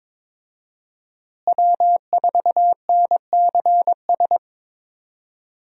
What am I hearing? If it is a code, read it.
Morse code, heard as a keyed tone, about 22 wpm, 700 Hz. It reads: W4NCS
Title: USA Call Signs - 5 Characters - Repeated 3x - 22wpm